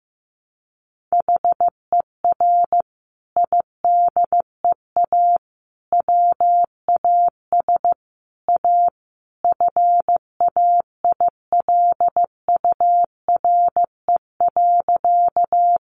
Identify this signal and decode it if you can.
Morse code, heard as a keyed tone, about 14 wpm, 700 Hz. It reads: HER IDEA WAS A FAILURE.
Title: Sentences from Top 900 Words 15wpm